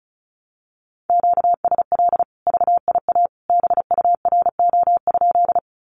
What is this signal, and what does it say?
Morse code, heard as a keyed tone, about 34 wpm, 700 Hz. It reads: QSL VIA BURO?
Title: Common QSO Phrases 35wpm